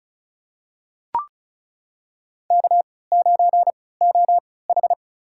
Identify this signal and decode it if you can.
Morse code, heard as a keyed tone, about 34 wpm, spaced as for 20 wpm, 700 Hz. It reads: K9OH
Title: USA Call Signs - 4 Characters - 3x Character Spacing - 35wpm